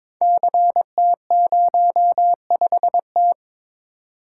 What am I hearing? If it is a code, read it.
Morse code, heard as a keyed tone, about 22 wpm, 700 Hz. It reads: CT05T
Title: Code Groups - 5 Letter:Numbers - 22wpm